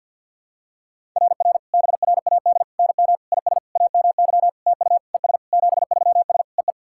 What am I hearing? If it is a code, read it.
Morse code, heard as a keyed tone, about 50 wpm, 700 Hz. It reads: RR BRAD NM IS AMY TU ES 73S EE
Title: SOTA QSO Practice 50wpm